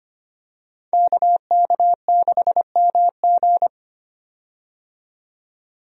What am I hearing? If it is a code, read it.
Morse code, heard as a keyed tone, about 25 wpm, 700 Hz. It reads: KK6MG
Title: USA Call Signs - 5 Characters - 25wpm